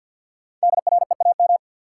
Morse code, heard as a keyed tone, about 50 wpm, 700 Hz. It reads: DREAM
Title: Sentences from Top 1500 Words - New Word Drill - 50wpm